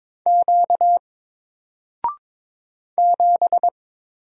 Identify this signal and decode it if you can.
Morse code, heard as a keyed tone, about 22 wpm, 700 Hz. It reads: Q 7
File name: Single-Letter-Number-22wpm